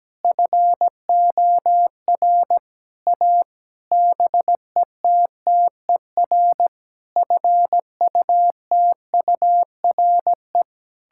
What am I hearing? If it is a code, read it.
Morse code, heard as a keyed tone, about 17 wpm, 700 Hz. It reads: FOR A BETTER FUTURE
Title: Sentences from Top 1000 Words - Review Part 6 - 17wpm